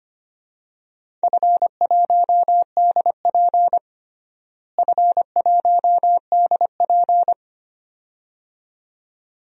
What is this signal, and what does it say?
Morse code, heard as a keyed tone, about 25 wpm, 700 Hz. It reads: F1DP F1DP
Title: DXCC-100 Call Signs - 4 Characters - Repeated 2x - 25wpm